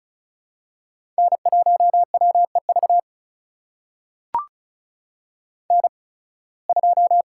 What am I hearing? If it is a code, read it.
Morse code, heard as a keyed tone, about 36 wpm, spaced as for 23 wpm, 700 Hz. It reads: N1WEV N2
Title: USA Call Signs - 5 Characters - 8x Character Spacing - 35wpm